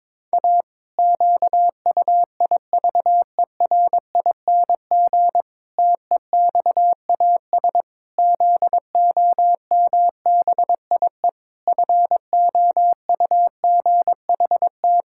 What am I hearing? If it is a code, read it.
Morse code, heard as a keyed tone, about 22 wpm, 700 Hz. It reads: A QUIVERING TEXAS ZOMBIE FOUGHT
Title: Pangram Challenge v2 - 22wpm